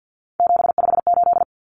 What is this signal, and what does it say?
Morse code, heard as a keyed tone, about 50 wpm, 700 Hz. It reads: 758
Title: Numbers - 3-Digit 50wpm